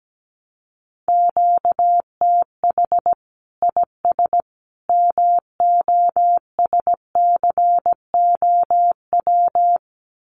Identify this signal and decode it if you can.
Morse code, heard as a keyed tone, about 17 wpm, 700 Hz. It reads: QTH IS MOSCOW
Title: Common QSO Phrases 17wpm